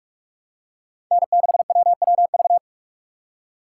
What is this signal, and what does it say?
Morse code, heard as a keyed tone, about 46 wpm, 700 Hz. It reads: N6WWV
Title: USA Call Signs - 5 Characters - 2x Character Spacing - 45wpm